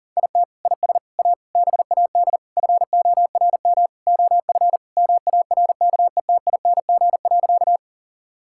Morse code, heard as a keyed tone, about 40 wpm, 700 Hz. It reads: IT IS A BAD FORM OF MARKETING.
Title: Sentences from Top 1000 Words - Review Part 5 - 40wpm